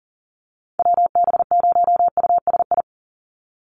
Morse code, heard as a keyed tone, about 40 wpm, 700 Hz. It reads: WB0USI
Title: NA SOTA Activator Callsigns 40wpm